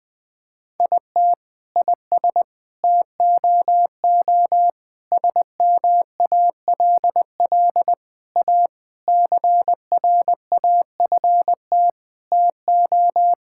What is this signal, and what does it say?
Morse code, heard as a keyed tone, about 19 wpm, 700 Hz. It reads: IT IS TOO SMALL A CRAFT TO
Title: Sentences from Top 1000 Words - Review Part 6 - 20wpm